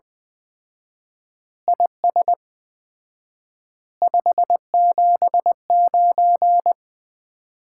Morse code, heard as a keyed tone, about 20 wpm, 700 Hz. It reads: IS 579
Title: Common QSO Phrases - 4x Word-Spacing - 20wpm